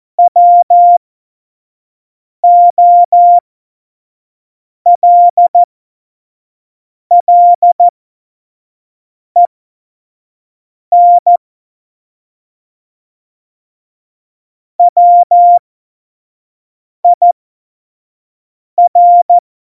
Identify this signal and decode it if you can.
Morse code, heard as a keyed tone, about 14 wpm, spaced as for 5 wpm, 700 Hz. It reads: WOLLEN WIR
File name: Morsekurs / Witze / Witz 17. Oktober 17, Tempo 25